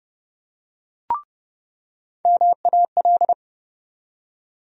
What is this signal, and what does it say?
Morse code, heard as a keyed tone, about 30 wpm, 700 Hz. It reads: MAL
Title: USA Call Sign Postfixes - 3 Letters 30wpm